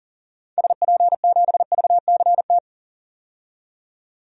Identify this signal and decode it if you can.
Morse code, heard as a keyed tone, about 40 wpm, 700 Hz. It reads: SP7VCT
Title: DXCC-100 Call Signs - 6 Characters - 40wpm